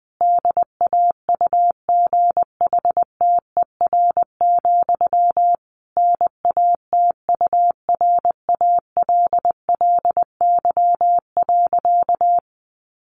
Morse code, heard as a keyed tone, about 20 wpm, 700 Hz. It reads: DAUGHTER, NATURALLY.
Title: Sentences from Top 1500 Words 20wpm